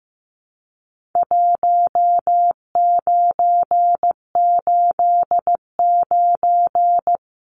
Morse code, heard as a keyed tone, about 15 wpm, 700 Hz. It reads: 1989
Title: Numbers - 4-Digit 15wpm